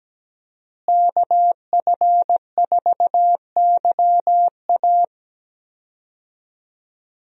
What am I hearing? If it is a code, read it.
Morse code, heard as a keyed tone, about 17 wpm, 700 Hz. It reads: KF4YA
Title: USA Call Signs - 5 Characters - Repeated 3x - 17wpm